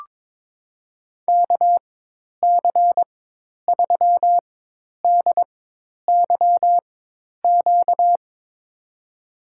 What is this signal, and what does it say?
Morse code, heard as a keyed tone, about 22 wpm, spaced as for 10 wpm, 700 Hz. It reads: KC3DYQ
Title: USA Call Signs - 4x Character Spacing - 22wpm